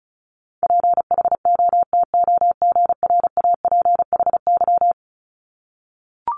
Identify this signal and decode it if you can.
Morse code, heard as a keyed tone, about 36 wpm, 700 Hz. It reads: PHOTOGRAPHY
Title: Multi-letter Phonemes 35wpm